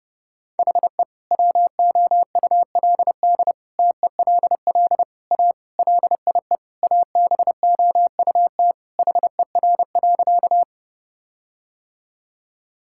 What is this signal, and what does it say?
Morse code, heard as a keyed tone, about 32 wpm, 700 Hz. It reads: HE WOULD TELL A LIE ABOUT HER.
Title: Sentences from Top 1600 Words 30wpm